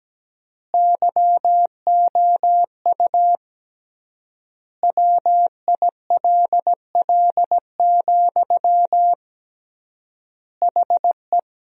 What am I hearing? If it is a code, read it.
Morse code, heard as a keyed tone, about 17 wpm, 700 Hz. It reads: YOU WILL, HE
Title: 3x Word-Spacing - Sentences from Top 200 Words - 17wpm